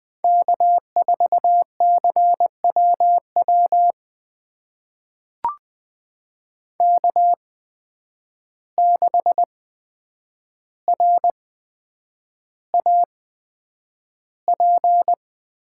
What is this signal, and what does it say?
Morse code, heard as a keyed tone, about 20 wpm, spaced as for 9 wpm, 700 Hz. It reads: K4CWW K6RAP
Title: USA Call Signs - 5 Characters - 8x Character Spacing - 20wpm